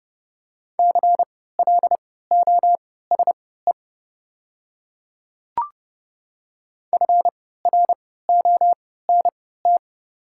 Morse code, heard as a keyed tone, about 30 wpm, spaced as for 17 wpm, 700 Hz. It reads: CLOSE FRONT
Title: 3x Character Spacing - Top 500 Words - 30wpm